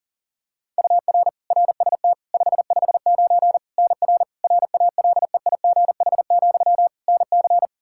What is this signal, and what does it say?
Morse code, heard as a keyed tone, about 37 wpm, 700 Hz. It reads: UR RST 559 NR RALEIGH, NC
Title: Practice QSO - Exchange 1 - Beginning - 40wpm